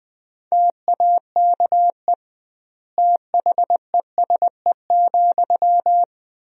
Morse code, heard as a keyed tone, about 20 wpm, 700 Hz. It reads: TAKE THESE,
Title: 2x Word-Spacing - Sentences from Top 300 Words - 20wpm